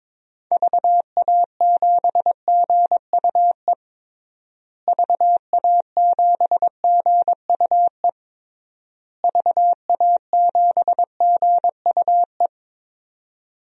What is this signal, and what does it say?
Morse code, heard as a keyed tone, about 22 wpm, 700 Hz. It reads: VA7GUE VA7GUE VA7GUE
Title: DXCC-100 Call Signs - 6 Characters - Repeated 3x - 22wpm